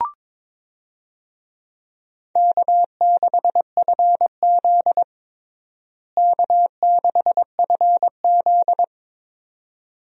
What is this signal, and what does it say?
Morse code, heard as a keyed tone, about 22 wpm, 700 Hz. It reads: K6FZ K6FZ
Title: USA Call Signs - 4 Characters - Repeated 2x - 22wpm